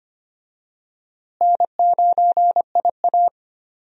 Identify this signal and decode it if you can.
Morse code, heard as a keyed tone, about 25 wpm, 700 Hz. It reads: N9IA
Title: USA Call Signs - 4 Characters - Repeated 2x - 25wpm